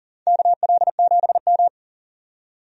Morse code, heard as a keyed tone, about 37 wpm, 700 Hz. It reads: KL7M
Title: DXCC-100 Call Signs - 4 Characters - 40wpm